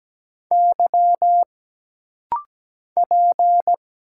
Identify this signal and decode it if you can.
Morse code, heard as a keyed tone, about 17 wpm, 700 Hz. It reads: Y P
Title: Single Letters - Rapid-Fire 17wpm